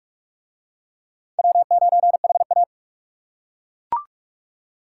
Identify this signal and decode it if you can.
Morse code, heard as a keyed tone, about 41 wpm, 700 Hz. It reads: W9HA